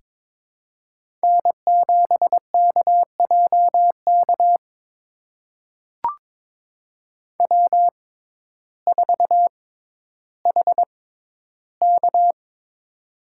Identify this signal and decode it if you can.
Morse code, heard as a keyed tone, about 22 wpm, spaced as for 12 wpm, 700 Hz. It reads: N7KJK W4HK
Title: USA Call Signs - 6x Character Spacing - 22wpm